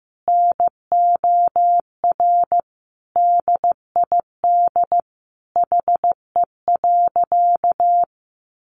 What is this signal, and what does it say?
Morse code, heard as a keyed tone, about 15 wpm, 700 Hz. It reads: NOR DID HE.
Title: Sentences from Top 1500 Words 15wpm